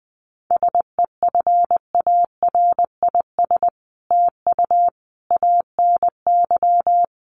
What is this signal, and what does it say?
Morse code, heard as a keyed tone, about 20 wpm, 700 Hz. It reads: SEFARIS TU ANY
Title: A Sound Of Thunder 20wpm